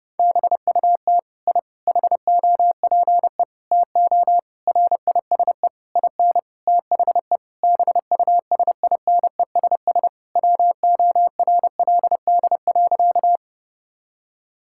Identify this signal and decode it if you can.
Morse code, heard as a keyed tone, about 30 wpm, 700 Hz. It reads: BUT I HOPE TO RISE IN THE BUSINESS WORLD.
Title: Sentences from Top 1000 Words - Review Part 6 - 30wpm